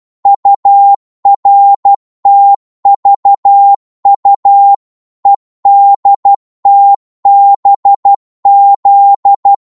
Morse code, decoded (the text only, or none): URTVU EDTBZ